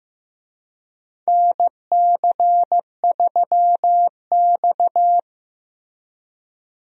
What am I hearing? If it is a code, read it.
Morse code, heard as a keyed tone, about 15 wpm, 700 Hz. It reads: NC3X
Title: USA Call Signs - 4 Characters - Repeated 2x - 15wpm